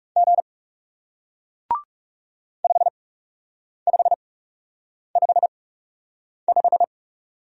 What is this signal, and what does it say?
Morse code, heard as a keyed tone, about 40 wpm, spaced as for 8 wpm, 700 Hz. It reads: G 5555